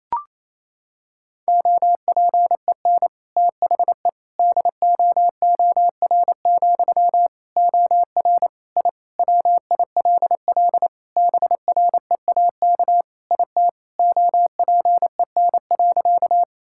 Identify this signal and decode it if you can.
Morse code, heard as a keyed tone, about 28 wpm, 700 Hz. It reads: OPEN THE DOOR, OR I WILL BREAK IT OPEN.
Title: Sentences from Top 1000 Words - Review Part 6 - 28wpm